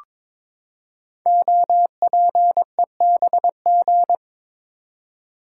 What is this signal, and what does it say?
Morse code, heard as a keyed tone, about 22 wpm, 700 Hz. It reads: OPEBG
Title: Code Groups - 5 Letter:Numbers - 22wpm